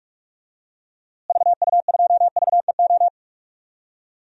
Morse code, heard as a keyed tone, about 45 wpm, 700 Hz. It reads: VU2VEO
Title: DXCC-100 Call Signs - 6 Characters - Repeated 2x - 45wpm